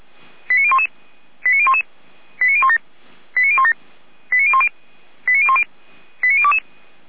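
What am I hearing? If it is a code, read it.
Five-tone selective call (ZVEI), heard as quick runs of five tones, tones of about 70 ms. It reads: 89011 89011 89017 89017 89010 89010 89022
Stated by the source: Five-tone ZVEI selective call received on SDR, recorded by fire brigade Feuerwehr Orlamünde.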